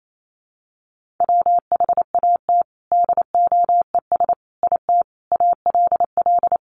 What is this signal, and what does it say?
Morse code, heard as a keyed tone, about 28 wpm, 700 Hz. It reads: WHAT DOES IT ALL
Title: Sentences from Top 200 Words 28wpm